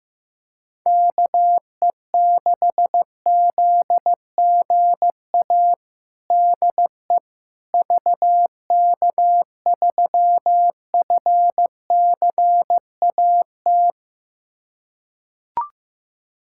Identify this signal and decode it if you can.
Morse code, heard as a keyed tone, about 15 wpm, 700 Hz. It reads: KE6ZGA DE VK3FCAT
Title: USA Callsign DE Callsign 15wpm